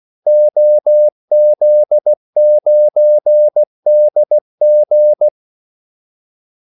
Morse code, heard as a keyed tone, about 16 wpm, 600 Hz. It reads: OZ9DG
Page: Kaldesignaler 16 wpm